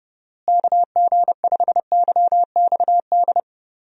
Morse code, heard as a keyed tone, about 30 wpm, 700 Hz. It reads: KG5YXD
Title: USA Call Signs - 6 Characters - Repeated 3x - 30wpm